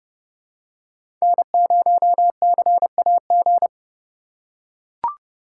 Morse code, heard as a keyed tone, about 30 wpm, 700 Hz. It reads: N0CAG
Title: USA Call Signs - 5 Characters - 4x Character Spacing - 30wpm